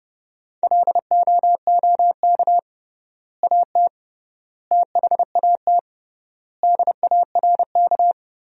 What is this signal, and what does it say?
Morse code, heard as a keyed tone, about 30 wpm, 700 Hz. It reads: LOOK AT THAT DARK
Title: 3x Word-Spacing - Sentences from Top 500 Words - 30wpm